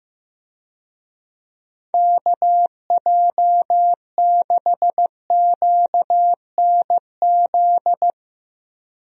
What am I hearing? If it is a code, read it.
Morse code, heard as a keyed tone, about 15 wpm, 700 Hz. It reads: KJ6QNZ
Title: USA Call Signs - 6 Characters - Repeated 2x - 15wpm